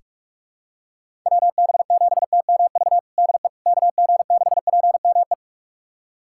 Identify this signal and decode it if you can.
Morse code, heard as a keyed tone, about 48 wpm, 700 Hz. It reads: WB7TMV DE KG6PME
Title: USA Callsign DE Callsign 45wpm